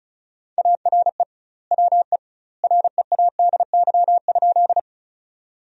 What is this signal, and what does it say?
Morse code, heard as a keyed tone, about 36 wpm, 700 Hz. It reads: ARE WE READY?
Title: 2x Word-Spacing - Sentences from Top 400 Words - 35wpm